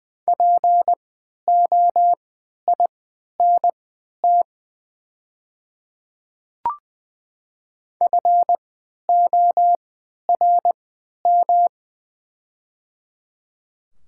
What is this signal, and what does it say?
Morse code, heard as a keyed tone, about 20 wpm, spaced as for 11 wpm, 700 Hz. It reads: POINT FORM
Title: 3x Character Spacing - Top 200 Words - 20wpm